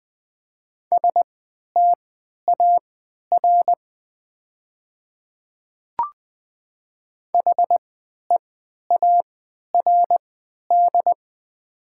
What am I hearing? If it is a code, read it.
Morse code, heard as a keyed tone, about 20 wpm, spaced as for 11 wpm, 700 Hz. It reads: STAR HEARD
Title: 3x Character Spacing - Top 500 Words - 20wpm